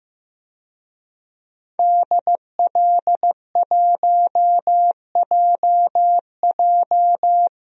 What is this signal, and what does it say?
Morse code, heard as a keyed tone, about 15 wpm, 700 Hz. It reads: DL1JJ
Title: DXCC-100 Call Signs - 5 Characters - 15wpm